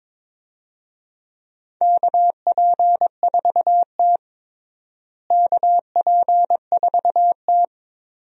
Morse code, heard as a keyed tone, about 21 wpm, 700 Hz. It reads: KP4T KP4T
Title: DXCC-100 Call Signs - 4 Characters - Repeated 3x - 22wpm